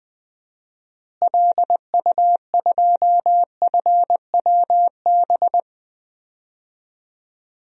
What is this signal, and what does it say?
Morse code, heard as a keyed tone, about 20 wpm, 700 Hz. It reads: LU2FWB
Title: DXCC-100 Call Signs - 6 Characters - Repeated 2x - 20wpm